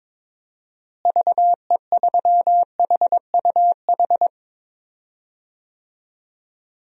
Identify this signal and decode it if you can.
Morse code, heard as a keyed tone, about 22 wpm, 700 Hz. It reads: VE3HUH
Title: DXCC-100 Call Signs - 6 Characters - Repeated 3x - 22wpm